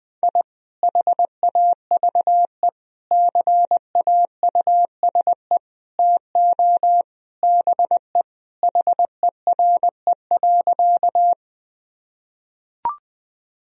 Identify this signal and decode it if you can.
Morse code, heard as a keyed tone, about 20 wpm, 700 Hz. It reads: I HAVE CAUSE TO BE HERE.
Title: Sentences from Top 1000 Words - Review Part 1 - 20wpm